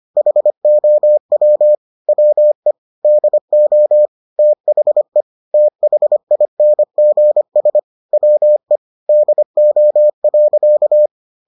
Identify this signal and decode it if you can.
Morse code, heard as a keyed tone, about 25 wpm, 600 Hz. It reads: HOW WE DO THE THINGS WE DO.